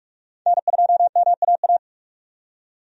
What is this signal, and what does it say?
Morse code, heard as a keyed tone, about 47 wpm, 700 Hz. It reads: N2MAA
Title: USA Call Signs - 5 Characters - 45wpm